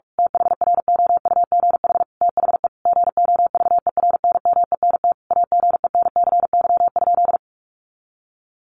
Morse code, heard as a keyed tone, about 44 wpm, 700 Hz. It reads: THROUGH THE GOVERNMENT AGENCY?
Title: Sentences from Top 1300 Words 45wpm